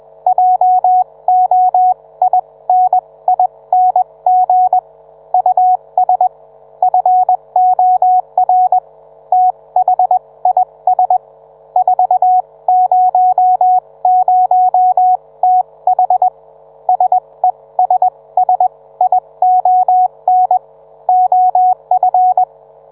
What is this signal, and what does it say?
Morse code, heard as a keyed tone, about 21 wpm, spaced as for 18 wpm, 740 Hz. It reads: JOINING US FOR THIS 400TH SESSION OF